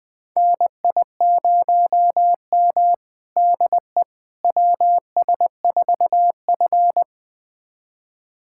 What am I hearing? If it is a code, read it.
Morse code, heard as a keyed tone, about 20 wpm, 700 Hz. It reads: NI0M DE WS4F
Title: USA Callsign DE Callsign - 4 Characters - 20wpm